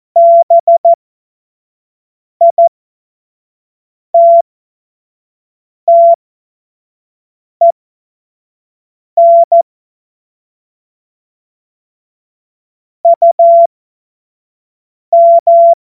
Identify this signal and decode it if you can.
Morse code, heard as a keyed tone, about 14 wpm, spaced as for 5 wpm, 680 Hz. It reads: BITTEN UM
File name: Witz-21-Oktober-2025-25-F70